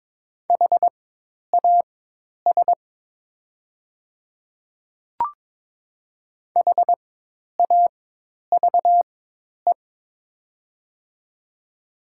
Morse code, heard as a keyed tone, about 22 wpm, spaced as for 10 wpm, 700 Hz. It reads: HAS HAVE
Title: 4x Character Spacing - Top 200 Words - 22wpm